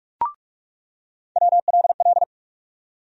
Morse code, heard as a keyed tone, about 46 wpm, 700 Hz. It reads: WLL
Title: USA Call Sign Postfixes - 3 Letters 45wpm